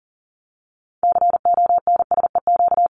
Morse code, heard as a keyed tone, about 40 wpm, 700 Hz. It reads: CONSEQ
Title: LY Suffix - Top 100 Words with Base Words 40wpm